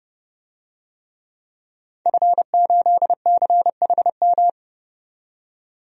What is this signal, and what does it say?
Morse code, heard as a keyed tone, about 32 wpm, 700 Hz. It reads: F8CHM
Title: DXCC-100 Call Signs - 5 Characters - 30wpm